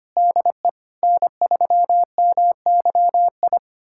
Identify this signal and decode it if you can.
Morse code, heard as a keyed tone, about 25 wpm, 700 Hz. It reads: DE N3MYI
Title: USA Callsign DE Callsign - 5 Characters - 25wpm